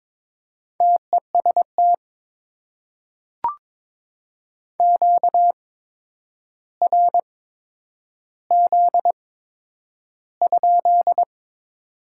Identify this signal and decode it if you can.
Morse code, heard as a keyed tone, about 22 wpm, spaced as for 9 wpm, 690 Hz. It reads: TEST QRZ?
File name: Top 90 QSO Elements - 8x Character Spacing - 22wpm